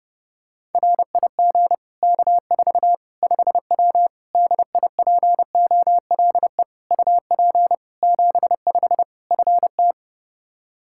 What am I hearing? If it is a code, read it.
Morse code, heard as a keyed tone, about 32 wpm, 700 Hz. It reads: RIG K4 5W DIPOLE UP 75 FT
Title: Practice QSO - Exchange 2 - Info - 30wpm